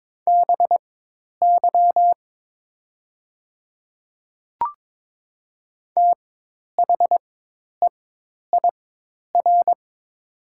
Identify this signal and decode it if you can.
Morse code, heard as a keyed tone, about 22 wpm, spaced as for 10 wpm, 700 Hz. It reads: BY THEIR
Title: 4x Character Spacing - Top 100 Words - 22wpm